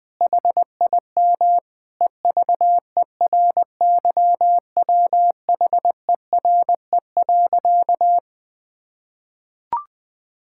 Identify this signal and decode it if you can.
Morse code, heard as a keyed tone, about 21 wpm, 700 Hz. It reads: HIM EVERYWHERE.
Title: Sentences from Top 1600 Words 20wpm